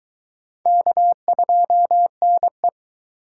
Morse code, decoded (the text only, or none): K2NE